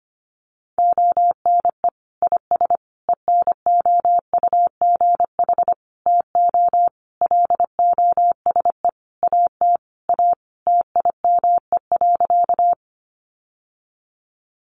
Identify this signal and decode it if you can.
Morse code, heard as a keyed tone, about 25 wpm, 700 Hz. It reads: ONE IS ENOUGH TO LOSE AT A TIME.
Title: Sentences from Top 1100 Words 25wpm